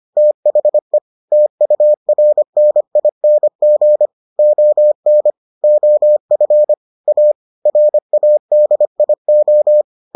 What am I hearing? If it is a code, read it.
Morse code, heard as a keyed tone, about 25 wpm, 600 Hz. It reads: THE TURNING ON OF A RADIO